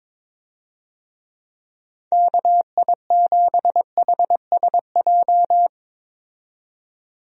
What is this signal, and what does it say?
Morse code, heard as a keyed tone, about 22 wpm, 700 Hz. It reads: KI7HSJ
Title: USA Call Signs - 6 Characters - 22wpm